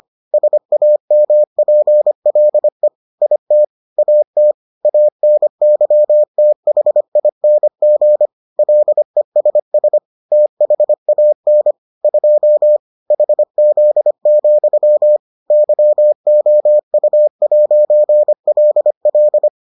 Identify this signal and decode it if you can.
Morse code, heard as a keyed tone, about 25 wpm, 600 Hz. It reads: SAMPLE IT AT ANYTHING LESS THAN 2 HZ, YOU'LL